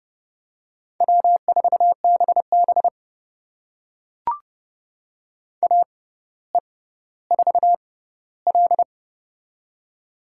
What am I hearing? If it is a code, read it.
Morse code, heard as a keyed tone, about 29 wpm, spaced as for 15 wpm, 700 Hz. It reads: W4BB AE4L
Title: USA Call Signs - 4 Characters - 6x Character Spacing - 30wpm